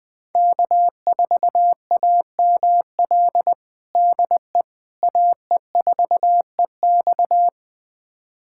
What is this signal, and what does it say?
Morse code, heard as a keyed tone, about 21 wpm, 700 Hz. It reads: K4AML DE AE4EX
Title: USA Callsign DE Callsign - 5 Characters - 20wpm